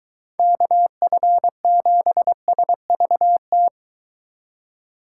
Morse code, heard as a keyed tone, about 22 wpm, 700 Hz. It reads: KF7SVT